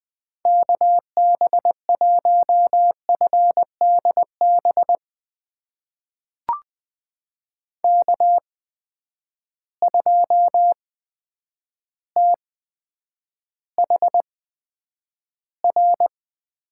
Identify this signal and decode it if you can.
Morse code, heard as a keyed tone, about 20 wpm, spaced as for 9 wpm, 700 Hz. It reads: KB1FDB K2THR
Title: USA Call Signs - 8x Character Spacing - 20wpm